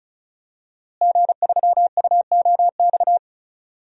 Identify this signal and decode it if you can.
Morse code, heard as a keyed tone, about 34 wpm, 700 Hz. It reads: G3UOX
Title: DXCC-100 Call Signs - 5 Characters - Repeated 2x - 35wpm